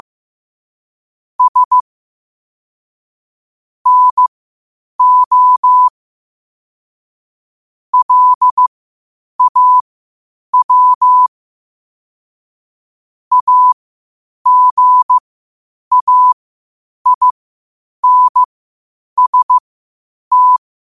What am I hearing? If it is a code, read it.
Morse code, heard as a keyed tone, about 15 wpm, spaced as for 8 wpm, 1000 Hz. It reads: S NO LAW AGAINST
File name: qotdfiles / 8wpm